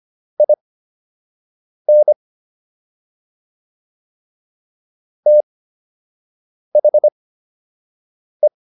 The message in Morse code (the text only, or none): IN THE